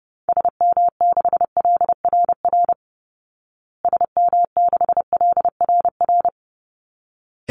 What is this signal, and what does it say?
Morse code, heard as a keyed tone, about 30 wpm, 700 Hz. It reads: SM6LRR SM6LRR
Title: Call Signs Repeated 3x - 30wpm